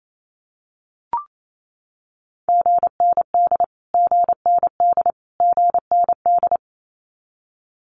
Morse code, heard as a keyed tone, about 28 wpm, 700 Hz. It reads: GND GND GND
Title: Abbreviations Repeated 3x - 28wpm